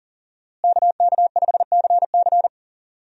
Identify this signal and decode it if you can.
Morse code, heard as a keyed tone, about 37 wpm, 700 Hz. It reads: KK5CC
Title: USA Call Signs - 5 Characters - Repeated 3x - 40wpm